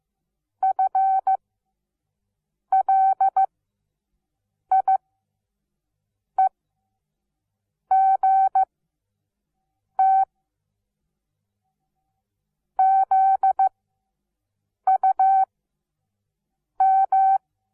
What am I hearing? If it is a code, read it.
Morse code, heard as a keyed tone, about 15 wpm, spaced as for 5 wpm, 770 Hz. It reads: FLIEGT ZUM
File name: Onlinelehrgang / morsen / DJ4UF Morsen17